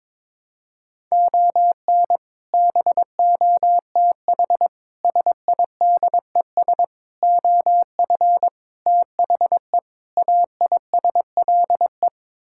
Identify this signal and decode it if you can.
Morse code, heard as a keyed tone, about 22 wpm, 700 Hz. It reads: ON BOTH SIDES OF THE AISLE
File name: practice / 22wpm / Sets of 6 Words - 22wpm